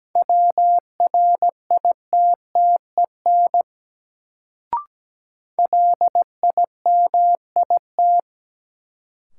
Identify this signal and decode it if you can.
Morse code, heard as a keyed tone, about 17 wpm, 700 Hz. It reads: WRITTEN LIMIT
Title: Sentences from Top 1500 Words - New Word Drill - 17wpm